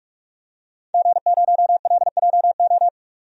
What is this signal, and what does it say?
Morse code, heard as a keyed tone, about 42 wpm, 700 Hz. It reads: G0LJO
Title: DXCC-100 Call Signs - 5 Characters - 45wpm